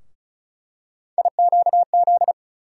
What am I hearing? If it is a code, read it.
Morse code, heard as a keyed tone, about 34 wpm, 700 Hz. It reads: IQZ
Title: USA Call Sign Postfixes - 3 Letters 35wpm